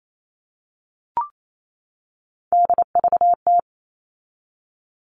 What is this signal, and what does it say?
Morse code, heard as a keyed tone, about 28 wpm, 700 Hz. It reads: DVT
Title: All Cut Numbers - 3-Digit 28wpm